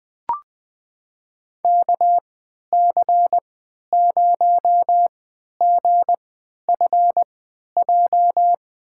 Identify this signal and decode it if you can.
Morse code, heard as a keyed tone, about 20 wpm, spaced as for 11 wpm, 700 Hz. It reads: KC0GFJ
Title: USA Call Signs - 6 Characters - 3x Character Spacing - 20wpm